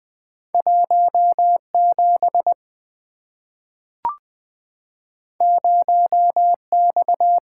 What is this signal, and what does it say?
Morse code, heard as a keyed tone, about 20 wpm, 700 Hz. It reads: 1 7 0 X
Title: Two Letter-Number Combinations - 20wpm